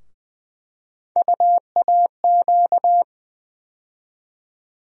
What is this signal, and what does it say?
Morse code, heard as a keyed tone, about 20 wpm, 700 Hz. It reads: UAQ
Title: USA Call Sign Postfixes - 3 Letters 20wpm